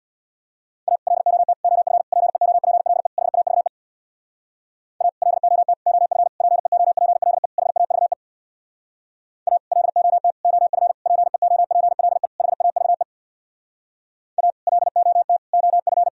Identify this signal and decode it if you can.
Morse code, heard as a keyed tone, about 64 wpm, 700 Hz. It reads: A LOT OF PEOPLE HAVE A LOT OF PEOPLE HAVE A LOT OF PEOPLE HAVE A LOT OF